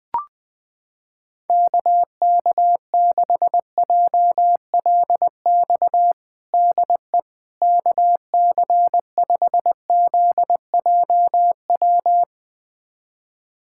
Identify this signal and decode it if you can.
Morse code, heard as a keyed tone, about 20 wpm, 700 Hz. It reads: KK6JLX DE KC5ZJW
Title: USA Callsign DE Callsign 20wpm